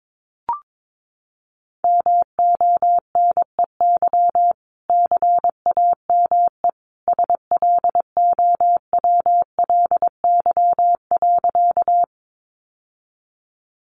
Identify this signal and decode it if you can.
Morse code, heard as a keyed tone, about 22 wpm, 700 Hz. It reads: MONEY CAME SLOWLY.
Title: Sentences from Top 1100 Words 22wpm